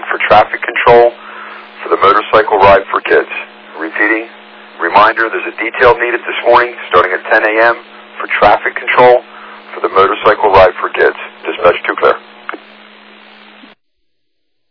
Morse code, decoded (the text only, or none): none